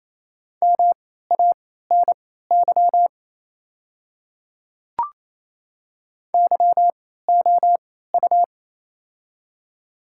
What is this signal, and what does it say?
Morse code, heard as a keyed tone, about 27 wpm, spaced as for 16 wpm, 690 Hz. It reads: MANY YOU
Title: 3x Character Spacing - Top 100 Words - 28wpm